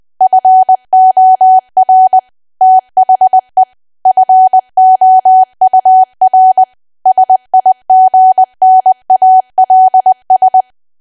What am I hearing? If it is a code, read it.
Morse code, heard as a keyed tone, about 20 wpm, 750 Hz. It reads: FOR THE FOUR SIGNALS
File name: cwa / W1AW-Level3 / 20WPM